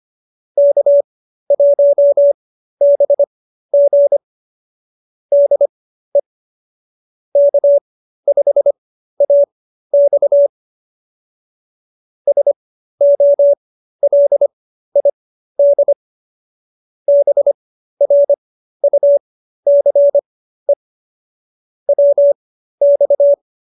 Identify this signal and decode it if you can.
Morse code, heard as a keyed tone, about 25 wpm, spaced as for 13 wpm, 570 Hz. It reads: K1BG DE K5AX SOLID BRUCE WX